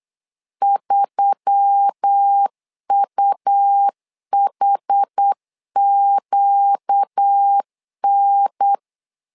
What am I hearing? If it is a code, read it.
Morse code, heard as a keyed tone, about 8 wpm, 800 Hz. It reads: 3UHQN
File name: LEZIONI / 5 LBVC / 5 LBVC 8wpm 2935